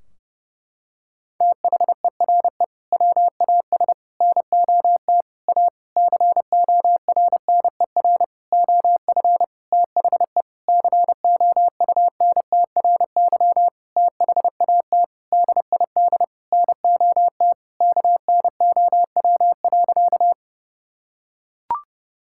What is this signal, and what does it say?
Morse code, heard as a keyed tone, about 30 wpm, 700 Hz. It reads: THERE WAS NOT A CORNER OF THE COUNTRY THAT DID NOT KNOW.
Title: Sentences from Top 1600 Words 30wpm